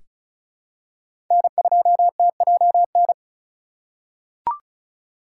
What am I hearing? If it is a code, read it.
Morse code, heard as a keyed tone, about 36 wpm, 700 Hz. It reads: N2TJN
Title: USA Call Signs - 5 Characters - 4x Character Spacing - 35wpm